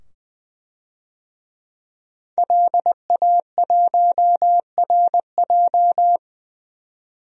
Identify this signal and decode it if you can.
Morse code, heard as a keyed tone, about 20 wpm, 700 Hz. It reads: LA1RJ
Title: DXCC-100 Call Signs - 5 Characters - 20wpm